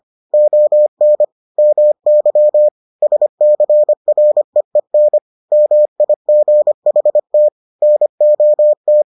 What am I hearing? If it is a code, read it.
Morse code, heard as a keyed tone, about 25 wpm, 600 Hz. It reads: ON MY SCREEN MIGHT NOT